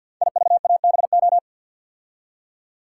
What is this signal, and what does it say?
Morse code, heard as a keyed tone, about 47 wpm, 700 Hz. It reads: IVABO
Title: Code Groups - 5 Letters - 50wpm